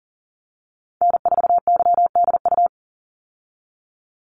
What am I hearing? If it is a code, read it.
Morse code, heard as a keyed tone, about 38 wpm, 700 Hz. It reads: N4YDU
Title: Call Signs - 40wpm